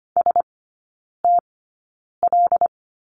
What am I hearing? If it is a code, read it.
Morse code, heard as a keyed tone, about 25 wpm, spaced as for 9 wpm, 700 Hz. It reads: STL
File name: course / 25:10wpm / 093 Lesson 16e - Letter c - Three Characters 25wpm